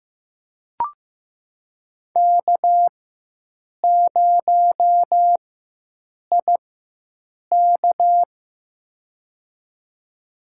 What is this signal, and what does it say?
Morse code, heard as a keyed tone, about 15 wpm, spaced as for 7 wpm, 700 Hz. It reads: K0IK
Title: USA Call Signs - 4 Characters - 4x Character Spacing - 15wpm